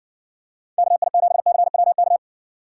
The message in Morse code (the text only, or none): KI7CWK